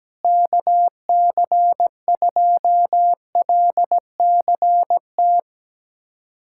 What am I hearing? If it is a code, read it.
Morse code, heard as a keyed tone, about 17 wpm, 700 Hz. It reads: KC2LCT
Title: USA Callsign DE Callsign 17wpm